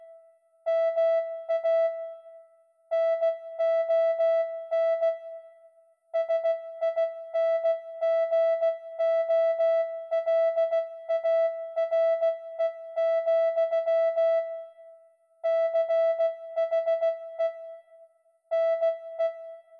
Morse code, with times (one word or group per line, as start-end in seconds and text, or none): 0.7-1.9 MA
2.9-5.1 NON
6.2-14.4 SINGOLARE,
15.5-17.5 CHE
18.5-19.3 NE